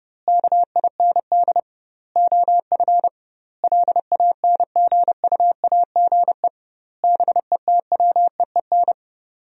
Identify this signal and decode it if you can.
Morse code, heard as a keyed tone, about 28 wpm, 700 Hz. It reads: KIND OF LANGUAGE BETWEEN
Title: 2x Word-Spacing - Sentences from Top 500 Words - 30wpm